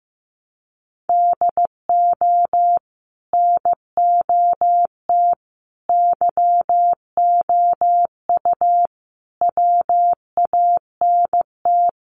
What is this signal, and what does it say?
Morse code, heard as a keyed tone, about 15 wpm, 700 Hz. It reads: DO NOT YOU WANT
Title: Sets of 5 Words Encore2 15wpm